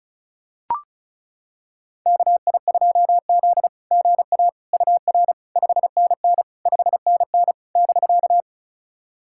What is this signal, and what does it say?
Morse code, heard as a keyed tone, about 36 wpm, 690 Hz. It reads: KI2Z GA UR 5NN 5NN <BK>
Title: SOTA QSO Practice 35wpm